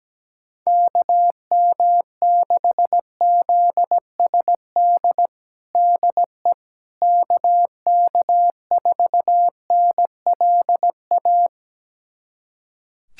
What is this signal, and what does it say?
Morse code, heard as a keyed tone, about 17 wpm, 700 Hz. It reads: KM6ZSD DE KK4NLA
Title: USA Callsign DE Callsign - 6 Characters - 17wpm